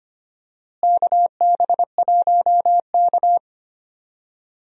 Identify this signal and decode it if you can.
Morse code, heard as a keyed tone, about 25 wpm, 700 Hz. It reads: KB1K
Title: USA Call Signs - 4 Characters - Repeated 3x - 25wpm